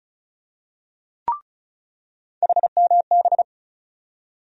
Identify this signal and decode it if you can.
Morse code, heard as a keyed tone, about 34 wpm, 700 Hz. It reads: HMB
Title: USA Call Sign Postfixes - 3 Letters 35wpm